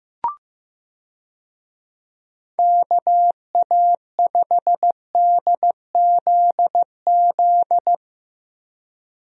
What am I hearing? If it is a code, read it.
Morse code, heard as a keyed tone, about 15 wpm, 700 Hz. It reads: KA5DZZ
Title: USA Call Signs - 6 Characters - Repeated 2x - 15wpm